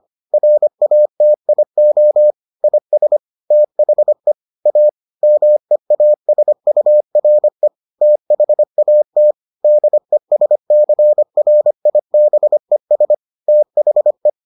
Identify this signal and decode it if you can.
Morse code, heard as a keyed tone, about 25 wpm, 600 Hz. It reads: RATIO IS THE A MEASURE THAT DESCRIBES THE